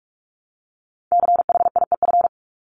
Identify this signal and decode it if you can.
Morse code, heard as a keyed tone, about 44 wpm, 700 Hz. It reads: CHIEF
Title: LY Suffix - Top 100 Words with Base Words 45wpm